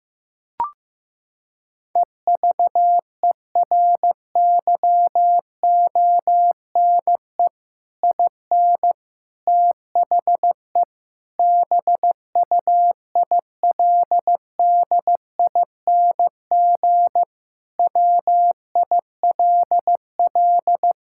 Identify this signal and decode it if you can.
Morse code, heard as a keyed tone, about 15 wpm, 700 Hz. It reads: EVERYONE IN THE BUILDING WILL
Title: Sentences from Top 1000 Words - Review Part 6 - 15wpm